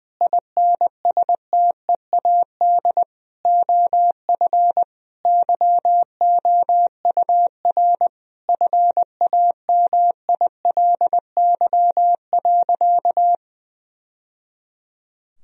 Morse code, heard as a keyed tone, about 21 wpm, 700 Hz. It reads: INSTEAD OF YOUR FAMILY.
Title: Sentences from Top 1000 Words - Review Part 4 - 20wpm